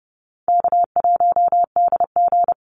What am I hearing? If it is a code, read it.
Morse code, heard as a keyed tone, about 30 wpm, 700 Hz. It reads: K1DG
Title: Easy Call Signs - 30wpm